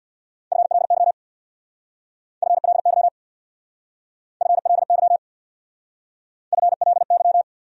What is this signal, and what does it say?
Morse code, heard as a keyed tone, about 62 wpm, 700 Hz. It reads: FLY FLY FLY FLY
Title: Speed-Racing Top 500 Words 50wpm